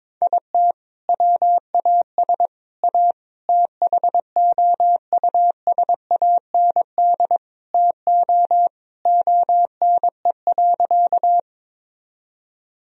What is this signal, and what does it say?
Morse code, heard as a keyed tone, about 22 wpm, 700 Hz. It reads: IT WAS A THOUSAND TO ONE.
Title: Sentences from Top 1000 Words - Review Part 3 - 22wpm